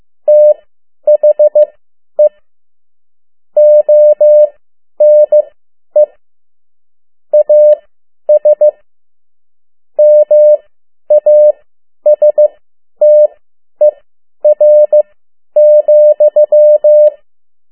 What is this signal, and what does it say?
Morse code, heard as a keyed tone, about 15 wpm, spaced as for 10 wpm, 600 Hz. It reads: THE ONE AS MASTER,
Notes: Around the World in 80 Days paragraph one Morse code 10 WPM